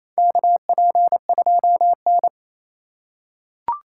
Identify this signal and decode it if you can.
Morse code, heard as a keyed tone, about 28 wpm, 700 Hz. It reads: KP2N
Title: K1USN SST QSO Practice - 28wpm